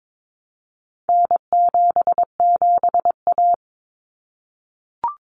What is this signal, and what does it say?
Morse code, heard as a keyed tone, about 22 wpm, 700 Hz. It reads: N77A
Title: Common Cut Numbers - 4-Digit 22wpm